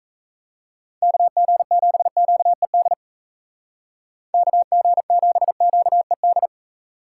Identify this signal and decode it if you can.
Morse code, heard as a keyed tone, about 34 wpm, 700 Hz. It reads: KG7QED KG7QED
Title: Speed Racing - USA 2x3 Call Signs 28wpm